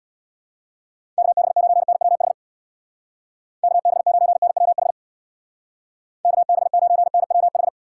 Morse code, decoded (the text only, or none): KB9NWH KB9NWH KB9NWH